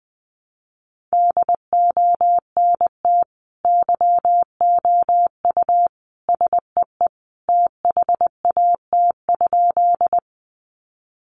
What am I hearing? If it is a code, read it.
Morse code, heard as a keyed tone, about 20 wpm, 700 Hz. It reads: DONT YOU SEE THAT?
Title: Sentences from Top 300 Words 20wpm